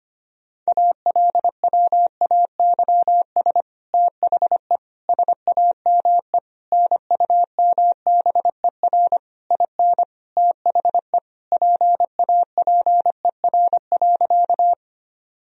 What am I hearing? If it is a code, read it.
Morse code, heard as a keyed tone, about 25 wpm, 700 Hz. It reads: ALWAYS THE SAME NUMBER IN THE PAPER.
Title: Sentences from Top 1000 Words - Review Part 1 - 25wpm